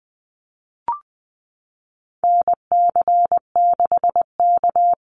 Morse code, heard as a keyed tone, about 20 wpm, 700 Hz. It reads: NC6K
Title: Call Signs - 20wpm